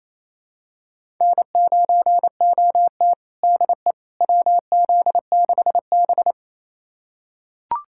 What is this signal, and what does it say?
Morse code, heard as a keyed tone, about 28 wpm, 700 Hz. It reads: N9OT DE WZ6B
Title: USA Callsign DE Callsign - 4 Characters - 28wpm